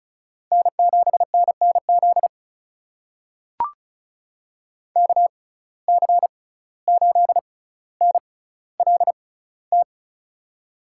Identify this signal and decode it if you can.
Morse code, heard as a keyed tone, about 36 wpm, spaced as for 17 wpm, 700 Hz. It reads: N7NNZ KC8NLT
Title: USA Call Signs - 6x Character Spacing - 35wpm